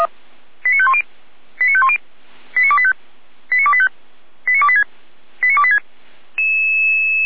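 Five-tone selective call (ZVEI), heard as quick runs of five tones, tones of about 70 ms. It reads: 89510 89510 89285 89285 89286 89286
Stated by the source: ZVEI five-tone selective call, code 89220, received by SDR.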